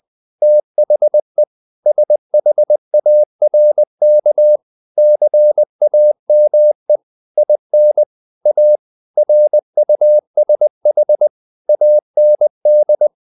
Morse code, heard as a keyed tone, about 20 wpm, 600 Hz. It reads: THE SHARK CAME IN A RUSH AND
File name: the old man and the sea part 2 20wpm